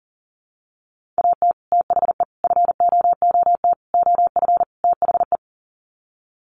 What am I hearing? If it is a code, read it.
Morse code, heard as a keyed tone, about 40 wpm, 700 Hz. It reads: AT THE FOOT OF THE
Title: Sets of 5 Words 40wpm